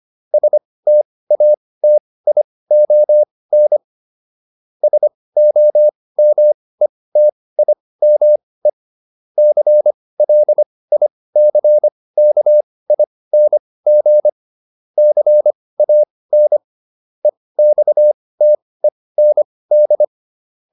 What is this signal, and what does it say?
Morse code, heard as a keyed tone, about 25 wpm, spaced as for 18 wpm, 600 Hz. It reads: STATION SOMETIME CLICKING CAN EXTEND